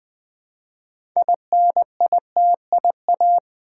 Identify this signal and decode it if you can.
Morse code, heard as a keyed tone, about 20 wpm, 700 Hz. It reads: INITIA
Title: Build A Word - 3 Letters At A Time - 8-Letter Words - 20wpm